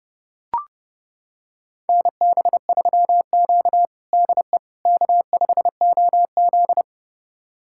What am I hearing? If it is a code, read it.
Morse code, heard as a keyed tone, about 30 wpm, 700 Hz. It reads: NB3Q DE K5OZ
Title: USA Callsign DE Callsign - 4 Characters - 30wpm